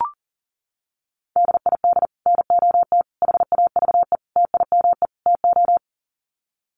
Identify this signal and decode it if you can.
Morse code, heard as a keyed tone, about 40 wpm, 700 Hz. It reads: DID NOT HAVE TIME TO
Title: Sets of 5 Words Encore 40wpm